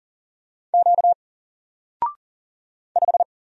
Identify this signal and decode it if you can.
Morse code, heard as a keyed tone, about 39 wpm, 700 Hz. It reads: Q 5
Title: Single Character (Letter:Number:Punctuation:Prosign) 40wpm